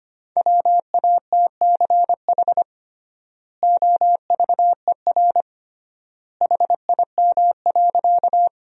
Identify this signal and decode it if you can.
Morse code, heard as a keyed tone, about 25 wpm, 700 Hz. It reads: WATCH OVER HIM.
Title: 3x Word-Spacing - Sentences from Top 300 Words - 25wpm